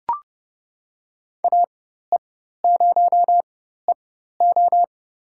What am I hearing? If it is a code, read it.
Morse code, heard as a keyed tone, about 29 wpm, spaced as for 14 wpm, 700 Hz. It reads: AE0EO
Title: USA Call Signs - 5 Characters - 4x Character Spacing - 30wpm